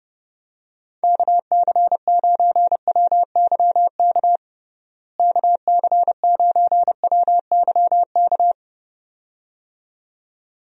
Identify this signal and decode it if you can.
Morse code, heard as a keyed tone, about 32 wpm, 700 Hz. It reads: KC9WYK KC9WYK
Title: USA Call Signs - 6 Characters - Repeated 2x - 30wpm